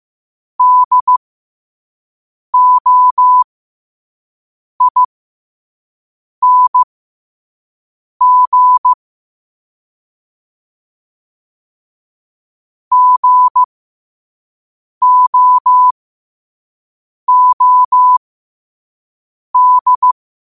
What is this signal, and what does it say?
Morse code, heard as a keyed tone, about 15 wpm, spaced as for 5 wpm, 1000 Hz. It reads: DOING GOOD
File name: 2026-02-15-5wpm